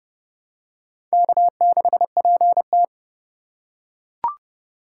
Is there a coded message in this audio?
Yes: Morse code from 1.1 to 2.8 s, heard as a keyed tone, about 27 wpm, 700 Hz. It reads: K6PT